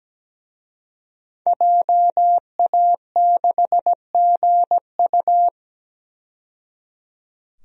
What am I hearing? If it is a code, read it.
Morse code, heard as a keyed tone, about 17 wpm, 700 Hz. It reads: JA6GU
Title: DXCC-100 Call Signs - 5 Characters - 17wpm